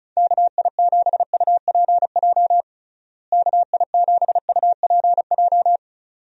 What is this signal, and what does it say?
Morse code, heard as a keyed tone, about 34 wpm, 700 Hz. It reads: KI7UPJ KI7UPJ
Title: USA Call Signs - 6 Characters - Repeated 2x - 35wpm